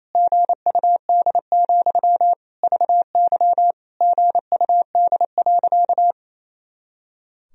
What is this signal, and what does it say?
Morse code, heard as a keyed tone, about 28 wpm, 700 Hz. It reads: GUD, VY GUD.
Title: Sentences from Top 200 Words and Abbreviations 28wpm